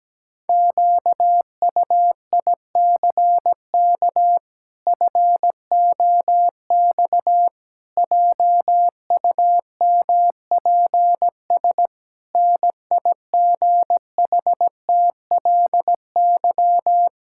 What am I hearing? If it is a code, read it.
Morse code, heard as a keyed tone, about 17 wpm, 700 Hz. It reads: QUICK FOX JUMPS NIGHTLY
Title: Pangram Challenge v2 - 17wpm